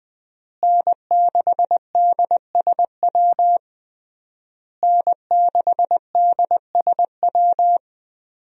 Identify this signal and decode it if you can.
Morse code, heard as a keyed tone, about 20 wpm, 700 Hz. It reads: N6DSW N6DSW
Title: USA Call Signs - 5 Characters - Repeated 3x - 20wpm